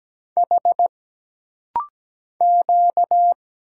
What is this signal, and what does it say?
Morse code, heard as a keyed tone, about 17 wpm, 700 Hz. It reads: H Q
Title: Single Letters - Rapid-Fire 17wpm